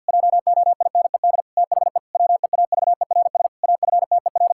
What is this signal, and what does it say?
Morse code, heard as a keyed tone, about 50 wpm, 700 Hz. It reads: JOINED THE WEAVERS AFTER
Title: Pangram Challenge v2 - 50wpm